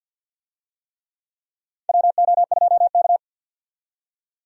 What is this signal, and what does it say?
Morse code, heard as a keyed tone, about 47 wpm, 700 Hz. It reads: WO2K
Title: USA Call Signs - 4 Characters - 50wpm